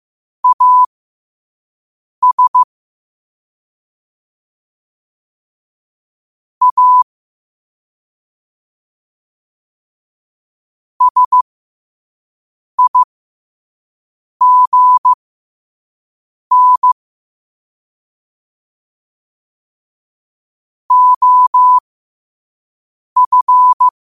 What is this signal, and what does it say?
Morse code, heard as a keyed tone, about 15 wpm, spaced as for 5 wpm, 1000 Hz. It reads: AS A SIGN OF
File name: qotdfiles / 5wpm